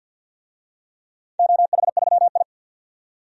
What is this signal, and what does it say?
Morse code, heard as a keyed tone, about 51 wpm, 700 Hz. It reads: OH3I
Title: DXCC-100 Call Signs - 4 Characters - 50wpm